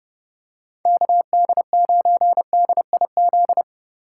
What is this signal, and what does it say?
Morse code, heard as a keyed tone, about 30 wpm, 700 Hz. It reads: KD9DIZ